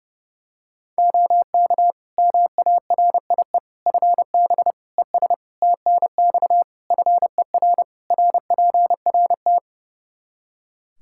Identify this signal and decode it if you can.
Morse code, heard as a keyed tone, about 32 wpm, 700 Hz. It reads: OK MARIE FB ES TNX FER RPRT
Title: Practice QSO - Exchange 2 - Info - 30wpm